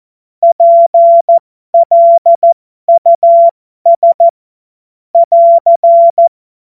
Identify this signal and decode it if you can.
Morse code, heard as a keyed tone, about 14 wpm, spaced as for 12 wpm, 680 Hz. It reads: PLUS <AR>
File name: Morsekurs / Witze